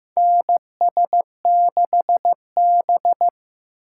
Morse code, heard as a keyed tone, about 15 wpm, 700 Hz. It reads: NS6B
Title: USA Call Signs - 3x Character Spacing - 15wpm